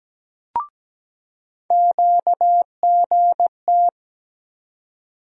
Morse code, heard as a keyed tone, about 17 wpm, 700 Hz. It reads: QGT
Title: USA Call Sign Postfixes - 3 Letters 17wpm